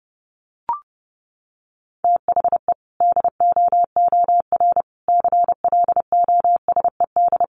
Morse code, heard as a keyed tone, about 30 wpm, 700 Hz. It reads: THE DOOR CLOSED
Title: Sentences from Top 1400 Words 30wpm